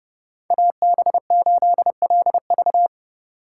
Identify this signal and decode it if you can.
Morse code, heard as a keyed tone, about 29 wpm, 700 Hz. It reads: AB8LV